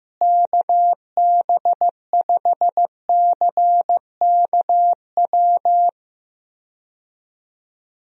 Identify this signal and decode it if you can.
Morse code, heard as a keyed tone, about 15 wpm, 700 Hz. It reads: KB5CKW
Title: USA Call Signs - 6 Characters - Repeated 2x - 15wpm